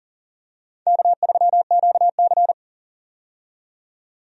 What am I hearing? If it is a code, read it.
Morse code, heard as a keyed tone, about 40 wpm, 700 Hz. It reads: K3QC
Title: USA Call Signs - 4 Characters - 40wpm